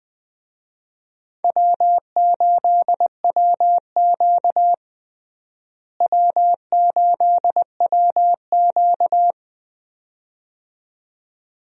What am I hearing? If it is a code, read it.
Morse code, heard as a keyed tone, about 20 wpm, 700 Hz. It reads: W8WQ W8WQ
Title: USA Call Signs - 4 Characters - Repeated 2x - 20wpm